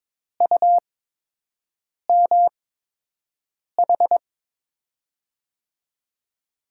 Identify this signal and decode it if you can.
Morse code, heard as a keyed tone, about 23 wpm, spaced as for 6 wpm, 700 Hz. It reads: UMH
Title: USA Call Signs - 8x Character Spacing - 22wpm